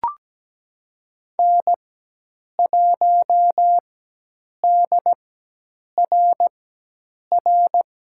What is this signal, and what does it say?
Morse code, heard as a keyed tone, about 17 wpm, spaced as for 8 wpm, 700 Hz. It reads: N1DRR
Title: USA Call Signs - 5 Characters - 4x Character Spacing - 17wpm